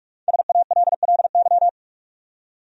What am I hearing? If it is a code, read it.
Morse code, heard as a keyed tone, about 44 wpm, 700 Hz. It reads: SALLY
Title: USA Common Names - 45wpm